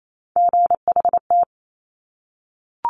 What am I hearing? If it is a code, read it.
Morse code, heard as a keyed tone, about 28 wpm, 700 Hz. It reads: GHT
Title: Top Three Letter Combinations in Words 28wpm